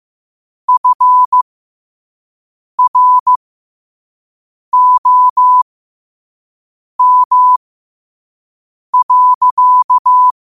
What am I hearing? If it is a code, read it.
Morse code, heard as a keyed tone, about 15 wpm, spaced as for 5 wpm, 1000 Hz. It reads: FROM.
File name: qotdfiles / 5wpm